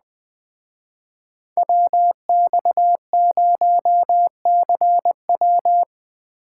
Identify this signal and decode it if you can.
Morse code, heard as a keyed tone, about 20 wpm, 700 Hz. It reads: WX0CW